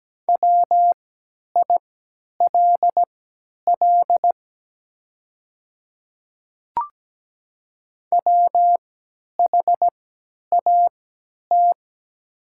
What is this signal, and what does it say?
Morse code, heard as a keyed tone, about 17 wpm, spaced as for 10 wpm, 700 Hz. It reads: WILL WHAT
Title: 3x Character Spacing - Top 100 Words - 17wpm